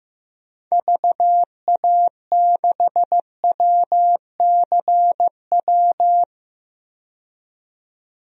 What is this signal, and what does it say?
Morse code, heard as a keyed tone, about 16 wpm, 700 Hz. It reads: VA6WCW
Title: DXCC-100 Call Signs - 6 Characters - 15wpm